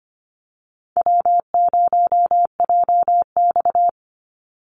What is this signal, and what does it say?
Morse code, heard as a keyed tone, about 25 wpm, 700 Hz. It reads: W0JX
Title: Easy Call Signs Repeated 2x - 25wpm